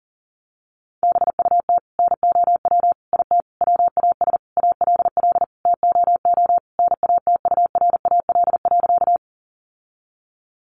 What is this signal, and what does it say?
Morse code, heard as a keyed tone, about 40 wpm, 700 Hz. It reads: BUT NOW IT WAS ALL TOO NATURAL.
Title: Sentences from Top 700 Words 40wpm